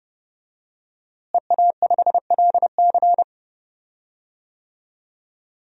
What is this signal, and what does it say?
Morse code, heard as a keyed tone, about 30 wpm, 700 Hz. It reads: EA5LC
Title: DXCC-100 Call Signs - 5 Characters - Repeated 2x - 30wpm